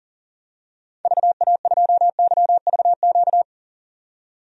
Morse code, heard as a keyed tone, about 38 wpm, 700 Hz. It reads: VA2YVQ
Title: DXCC-100 Call Signs - 6 Characters - 40wpm